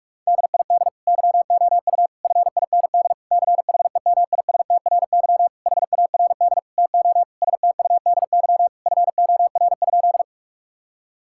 Morse code, heard as a keyed tone, about 46 wpm, 700 Hz. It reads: DID YOU FIND CHEMISTRY HARD TO STUDY FOR?
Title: Sentences from Top 1000 Words - Review Part 6 - 45wpm